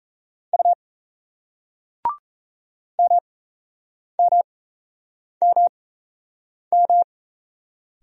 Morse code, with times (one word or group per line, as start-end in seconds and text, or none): 0.5-0.7 U
3.0-7.0 MMMM